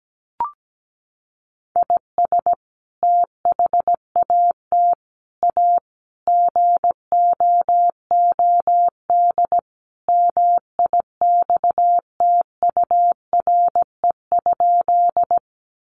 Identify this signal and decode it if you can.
Morse code, heard as a keyed tone, about 17 wpm, 700 Hz. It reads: IS THAT A GOOD MIXTURE?
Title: Sentences from Top 1500 Words 17wpm